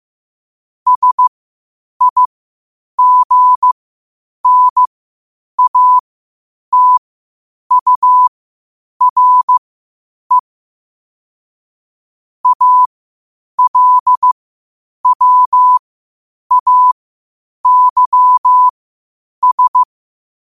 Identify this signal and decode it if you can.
Morse code, heard as a keyed tone, about 15 wpm, spaced as for 8 wpm, 1000 Hz. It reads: SIGNATURE ALWAYS
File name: qotdfiles / 8wpm